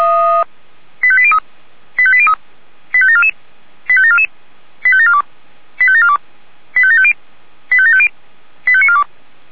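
Five-tone selective call (ZVEI), heard as quick runs of five tones, tones of about 70 ms. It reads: 85803 85803 85744 85744 85742 85742 85750 85750 85942
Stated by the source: ZVEI five-tone selective call received by SDR.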